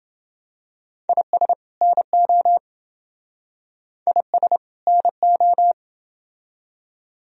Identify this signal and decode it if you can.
Morse code, heard as a keyed tone, about 28 wpm, 700 Hz. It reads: IS NO IS NO